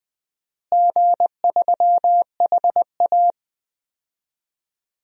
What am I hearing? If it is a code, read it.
Morse code, heard as a keyed tone, about 20 wpm, 700 Hz. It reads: G3HA
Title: DXCC-100 Call Signs - 4 Characters - Repeated 3x - 20wpm